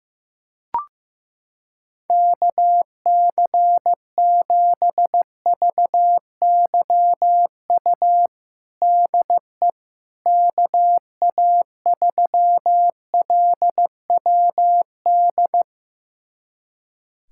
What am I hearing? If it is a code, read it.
Morse code, heard as a keyed tone, about 15 wpm, 700 Hz. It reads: KC7VYU DE KA3LWD
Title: USA Callsign DE Callsign 15wpm